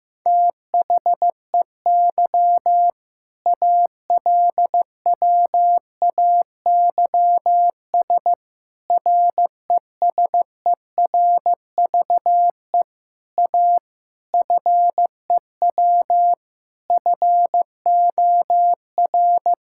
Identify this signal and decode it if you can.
Morse code, heard as a keyed tone, about 15 wpm, 700 Hz. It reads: THEY ALWAYS RESERVE A FEW FOR
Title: Sentences from Top 1000 Words - Review Part 6 - 15wpm